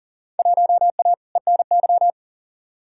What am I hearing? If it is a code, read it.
Morse code, heard as a keyed tone, about 37 wpm, 700 Hz. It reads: 1A ENY
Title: Field Day Exchanges - 40wpm